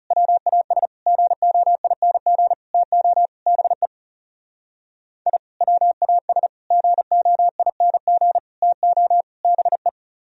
Morse code, heard as a keyed tone, about 36 wpm, 700 Hz. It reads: WAS GOING TO BE I WAS GOING TO BE
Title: Speed-Racing - Sets of 5 Words 30wpm